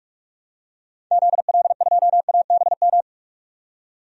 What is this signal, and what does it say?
Morse code, heard as a keyed tone, about 46 wpm, 700 Hz. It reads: ZL2ABM
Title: DXCC-100 Call Signs - 6 Characters - 45wpm